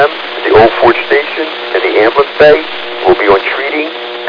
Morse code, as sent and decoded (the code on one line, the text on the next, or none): none